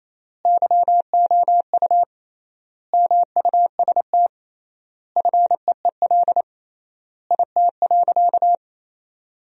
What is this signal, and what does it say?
Morse code, heard as a keyed tone, about 28 wpm, 700 Hz. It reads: YOU MUST FEEL IT.
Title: 3x Word-Spacing - Sentences from Top 400 Words - 28wpm